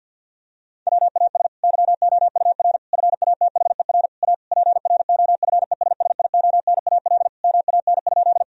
Morse code, heard as a keyed tone, about 50 wpm, 700 Hz. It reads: WAS YOUR FATHER A PROFESSIONAL MAN?
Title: Sentences from Top 1000 Words - Review Part 4 - 50wpm